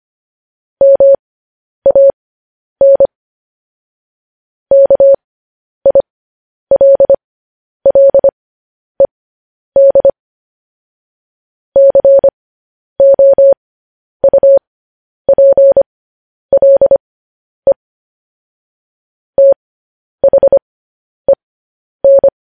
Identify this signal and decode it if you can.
Morse code, heard as a keyed tone, about 25 wpm, spaced as for 10 wpm, 550 Hz. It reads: MAN KILLED COUPLE THEN
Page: News Headlines in Morse Code at 10 WPM